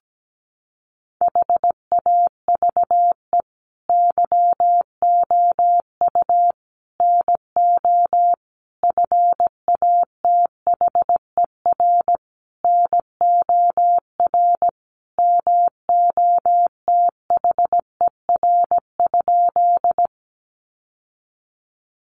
Morse code, heard as a keyed tone, about 17 wpm, 700 Hz. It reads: HAVE YOU NO FATHER NOR MOTHER?
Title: Sentences from Top 1500 Words 17wpm